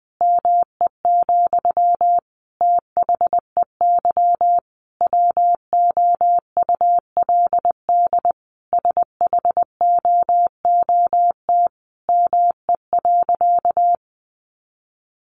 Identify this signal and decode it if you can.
Morse code, heard as a keyed tone, about 19 wpm, 700 Hz. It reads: ME, THEY WOULD SHOOT ME.
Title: Sentences from Top 1500 Words 20wpm